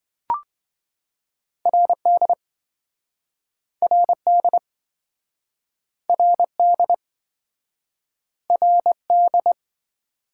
Morse code, heard as a keyed tone, about 27 wpm, 700 Hz. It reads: RD RD RD RD